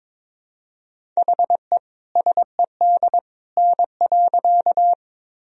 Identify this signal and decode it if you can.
Morse code, heard as a keyed tone, about 22 wpm, 700 Hz. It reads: HE SED N.
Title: Sentences from Top 100 Words and Abbreviations 22wpm